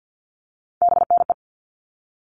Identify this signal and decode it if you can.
Morse code, heard as a keyed tone, about 50 wpm, 700 Hz. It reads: BNE
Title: All Cut Numbers - 3-Digit 50wpm